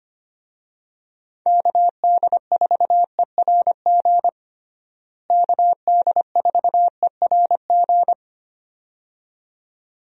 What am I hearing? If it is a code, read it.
Morse code, heard as a keyed tone, about 25 wpm, 700 Hz. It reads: KD4ERG KD4ERG
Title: USA Call Signs - 6 Characters - Repeated 2x - 25wpm